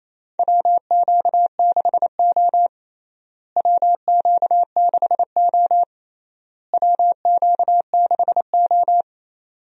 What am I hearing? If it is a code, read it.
Morse code, heard as a keyed tone, about 28 wpm, 700 Hz. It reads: WQ6O WQ6O WQ6O
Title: USA Call Signs - 4 Characters - Repeated 3x - 28wpm